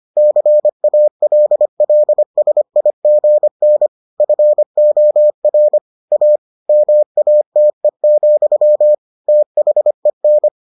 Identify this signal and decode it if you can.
Morse code, heard as a keyed tone, about 25 wpm, 600 Hz. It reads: CALLSIGN FOR A MATE, THEN